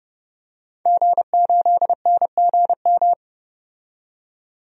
Morse code, heard as a keyed tone, about 30 wpm, 700 Hz. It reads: G8NGM
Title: DXCC-100 Call Signs - 5 Characters - 30wpm